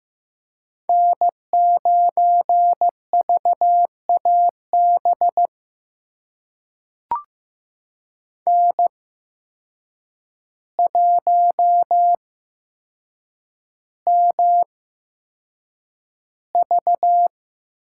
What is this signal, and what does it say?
Morse code, heard as a keyed tone, about 15 wpm, spaced as for 7 wpm, 700 Hz. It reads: N9VAB N1MV
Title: USA Call Signs - 8x Character Spacing - 15wpm